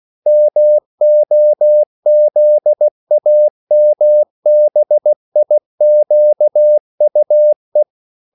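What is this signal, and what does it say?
Morse code, heard as a keyed tone, about 16 wpm, 600 Hz. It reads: MOZAMBIQUE
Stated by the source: CW practice of national capitals at 16 words per minute, from ON6ZQ.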